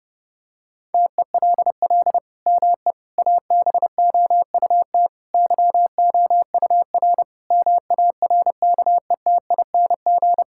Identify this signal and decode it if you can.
Morse code, heard as a keyed tone, about 30 wpm, 700 Hz. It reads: TELL ME ABOUT YOUR MARKETING
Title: Sentences from Top 1000 Words - Review Part 5 - 30wpm